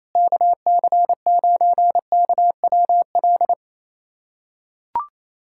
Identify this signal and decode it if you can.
Morse code, heard as a keyed tone, about 28 wpm, 700 Hz. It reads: KC9KWL
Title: USA Callsign DE Callsign - 6 Characters - 28wpm